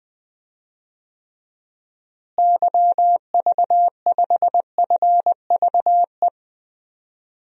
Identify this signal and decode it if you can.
Morse code, heard as a keyed tone, about 20 wpm, 700 Hz. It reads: YV5FVE
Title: DXCC-100 Call Signs - 6 Characters - Repeated 3x - 20wpm